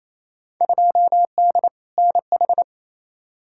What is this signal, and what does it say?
Morse code, heard as a keyed tone, about 27 wpm, 700 Hz. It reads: 2D NH
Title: Field Day Exchanges Encore - 28wpm